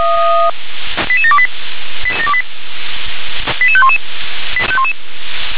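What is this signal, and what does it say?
Five-tone selective call (ZVEI), heard as quick runs of five tones, tones of about 70 ms. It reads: 88728 88511
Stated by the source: ZVEI five-tone call received by SDR.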